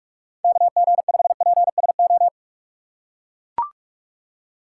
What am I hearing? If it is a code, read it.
Morse code, heard as a keyed tone, about 42 wpm, 700 Hz. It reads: KG5PSO
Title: USA Call Signs - 6 Characters - 3x Character Spacing - 45wpm